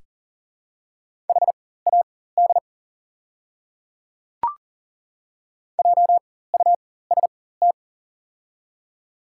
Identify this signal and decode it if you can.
Morse code, heard as a keyed tone, about 40 wpm, spaced as for 19 wpm, 690 Hz. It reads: HAD JUST
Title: 4x Character Spacing - Top 300 Words - 40wpm